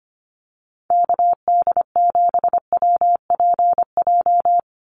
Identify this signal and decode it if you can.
Morse code, heard as a keyed tone, about 25 wpm, 700 Hz. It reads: KD7WPJ
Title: NA SOTA Activator Callsigns 25wpm